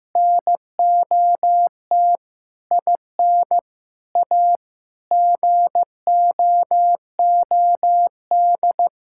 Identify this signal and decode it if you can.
Morse code, heard as a keyed tone, about 16 wpm, 700 Hz. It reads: NOT IN A GOOD
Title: Sentences from Top 1000 Words - Review Part 6 - 15wpm